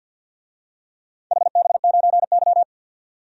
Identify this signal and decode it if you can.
Morse code, heard as a keyed tone, about 50 wpm, 700 Hz. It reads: HB9Y
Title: DXCC-100 Call Signs - 4 Characters - 50wpm